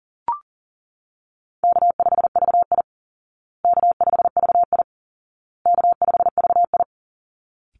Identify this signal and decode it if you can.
Morse code, heard as a keyed tone, about 40 wpm, 700 Hz. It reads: K5VI K5VI K5VI
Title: Call Signs Repeated 2x - 40wpm